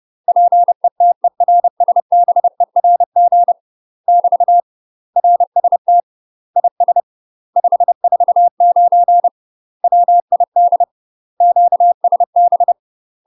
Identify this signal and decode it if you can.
Morse code, heard as a keyed tone, about 30 wpm, 700 Hz. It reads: PETERSBERG <BT> RST IS 549 WID QSB
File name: UK MASTER QSOs TRAINING / 30WPM / UK MASTER QSOs 0005